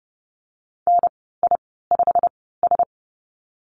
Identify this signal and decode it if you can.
Morse code, heard as a keyed tone, about 30 wpm, spaced as for 17 wpm, 700 Hz. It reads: NI5S
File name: Lesson 15g - Number 5 - Call Signs 30wpm